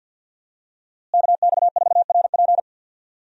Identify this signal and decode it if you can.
Morse code, heard as a keyed tone, about 50 wpm, 700 Hz. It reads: KX4RP
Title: USA Call Signs - 5 Characters - Repeated 2x - 50wpm